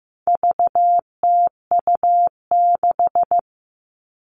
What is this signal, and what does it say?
Morse code, heard as a keyed tone, about 15 wpm, 700 Hz. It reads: VTU6
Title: All Cut Numbers - 4-Digit 15wpm